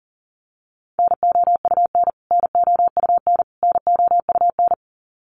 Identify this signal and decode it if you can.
Morse code, heard as a keyed tone, about 40 wpm, 700 Hz. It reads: NOUN NOUN NOUN
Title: Top 500 Words Repeated 40wpm